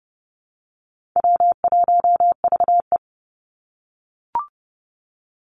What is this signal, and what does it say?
Morse code, heard as a keyed tone, about 30 wpm, 700 Hz. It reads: W1VE
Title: Call Signs - 30wpm